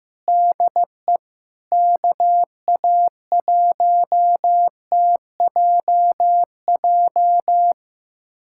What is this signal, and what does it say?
Morse code, heard as a keyed tone, about 16 wpm, 700 Hz. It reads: DE KA1TJJ
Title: USA Callsign DE Callsign - 6 Characters - 15wpm